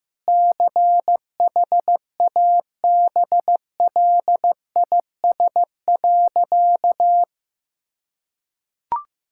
Morse code, heard as a keyed tone, about 15 wpm, 700 Hz. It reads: CHABLIS.
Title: Pangram Challenge v2 - 15wpm